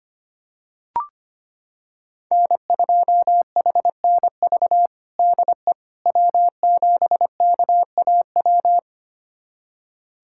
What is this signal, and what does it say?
Morse code, heard as a keyed tone, about 25 wpm, 700 Hz. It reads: N2HNV DE W7KAW
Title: USA Callsign DE Callsign - 5 Characters - 25wpm